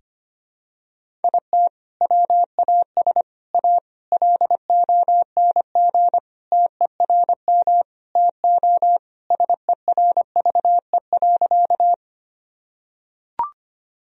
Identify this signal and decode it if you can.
Morse code, heard as a keyed tone, about 25 wpm, 700 Hz. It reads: IT WAS A LONG TERM TO SERVE.
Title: Sentences from Top 1000 Words - Review Part 4 - 25wpm